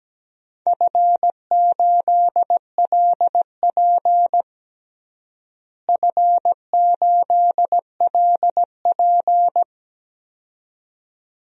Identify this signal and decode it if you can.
Morse code, heard as a keyed tone, about 17 wpm, 700 Hz. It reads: F8LP F8LP
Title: DXCC-100 Call Signs - 4 Characters - Repeated 2x - 17wpm